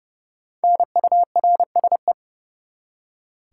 Morse code, heard as a keyed tone, about 29 wpm, 700 Hz. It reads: NURSE
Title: Five Letter Words - 30wpm